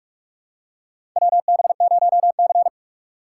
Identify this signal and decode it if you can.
Morse code, heard as a keyed tone, about 42 wpm, 700 Hz. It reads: WB0C
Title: USA Call Signs - 4x Character Spacing - 45wpm